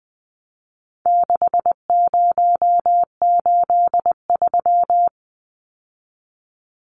Morse code, heard as a keyed tone, about 19 wpm, 700 Hz. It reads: 6083
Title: Numbers - 4-Digit 20wpm